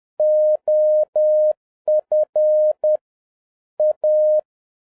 Morse code, heard as a keyed tone, about 10 wpm, 610 Hz. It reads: OF A
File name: 2011-07-mp3 / n21 GREETINGS FROM SINGAPORE 10wpm